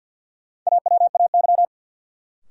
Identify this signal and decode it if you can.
Morse code, heard as a keyed tone, about 52 wpm, 700 Hz. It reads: AWAY
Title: Sets of 2 Words - New Word Drill - 50wpm